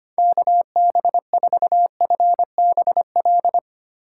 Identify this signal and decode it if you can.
Morse code, heard as a keyed tone, about 25 wpm, 700 Hz. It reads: KB4FBL
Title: USA Call Signs - 3x Character Spacing - 25wpm